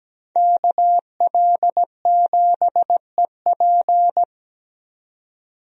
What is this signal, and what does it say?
Morse code, heard as a keyed tone, about 17 wpm, 700 Hz. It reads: KL7EP
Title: DXCC-100 Call Signs - 5 Characters - 17wpm